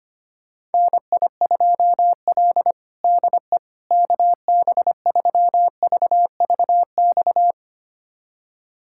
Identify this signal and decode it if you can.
Morse code, heard as a keyed tone, about 25 wpm, 700 Hz. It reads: NI2L DE KB3VVX
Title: USA Callsign DE Callsign 25wpm